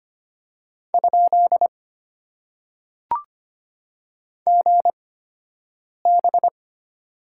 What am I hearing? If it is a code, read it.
Morse code, heard as a keyed tone, about 25 wpm, spaced as for 7 wpm, 690 Hz. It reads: ? GB
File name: Top 90 QSO Elements - 8x Character Spacing - 25wpm